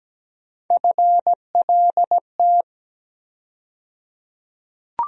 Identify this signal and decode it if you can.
Morse code, heard as a keyed tone, about 17 wpm, 700 Hz. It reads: FLT
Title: USA Call Sign Postfixes - 3 Letters 17wpm